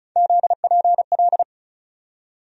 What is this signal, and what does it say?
Morse code, heard as a keyed tone, about 36 wpm, 700 Hz. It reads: ZPL
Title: USA Call Sign Postfixes - 3 Letters 35wpm